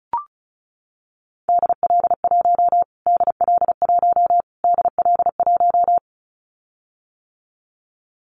Morse code, heard as a keyed tone, about 34 wpm, 700 Hz. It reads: DL1 DL1 DL1
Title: Callsign Prefixes for Top 25 DXCC - Repeated 3x - 35wpm